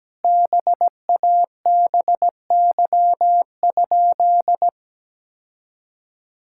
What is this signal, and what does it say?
Morse code, heard as a keyed tone, about 17 wpm, 700 Hz. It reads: BABY?
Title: Sentences from Top 1600 Words 17wpm